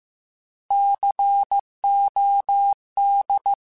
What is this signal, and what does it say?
Morse code, heard as a keyed tone, about 15 wpm, 800 Hz. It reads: COD